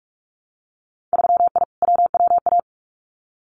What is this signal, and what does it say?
Morse code, heard as a keyed tone, about 46 wpm, 690 Hz. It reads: 3I WWA
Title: Winter Field Day Exchanges Encore 45wpm